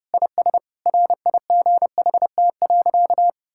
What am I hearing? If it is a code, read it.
Morse code, heard as a keyed tone, about 30 wpm, 700 Hz. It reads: IS RIGHT.
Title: Sentences from Top 1600 Words 30wpm